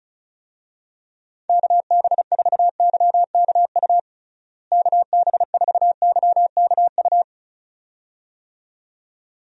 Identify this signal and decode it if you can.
Morse code, heard as a keyed tone, about 34 wpm, 700 Hz. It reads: KB4YKU KB4YKU
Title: USA Call Signs - 6 Characters - Repeated 2x - 35wpm